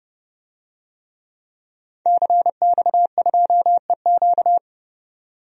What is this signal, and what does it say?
Morse code, heard as a keyed tone, about 30 wpm, 700 Hz. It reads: CX2EQ
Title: DXCC-100 Call Signs - 5 Characters - 30wpm